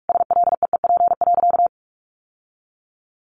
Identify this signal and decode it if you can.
Morse code, heard as a keyed tone, about 44 wpm, 700 Hz. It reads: SLEEP.
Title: Sentences from Top 1500 Words 45wpm